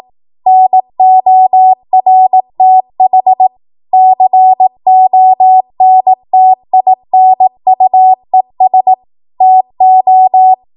Morse code, heard as a keyed tone, about 18 wpm, 750 Hz. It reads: NORTH CONTINUES TO
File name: Morse / 18WPM